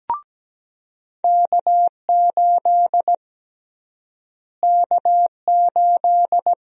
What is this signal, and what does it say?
Morse code, heard as a keyed tone, about 17 wpm, 700 Hz. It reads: K8 K8
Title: USA Call Sign Prefixes - Repeated 2x - 17wpm